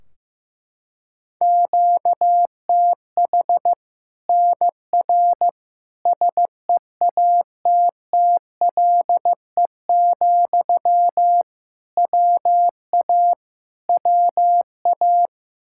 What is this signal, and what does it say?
Morse code, heard as a keyed tone, about 15 wpm, 700 Hz. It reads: QTH NR SEATTLE, WA WA
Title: Practice QSO - Exchange 1 - Beginning - 15wpm